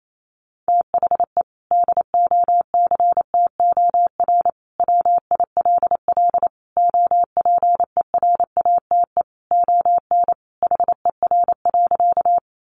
Morse code, heard as a keyed tone, about 28 wpm, 700 Hz. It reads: THE DOCTOR WILL OPERATE ON HER.
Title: Sentences from Top 1500 Words 28wpm